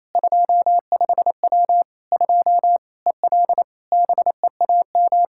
Morse code, heard as a keyed tone, about 27 wpm, 700 Hz. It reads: 25W 2 EL BEAM
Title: Practice QSO - Exchange 2 - Info - 28wpm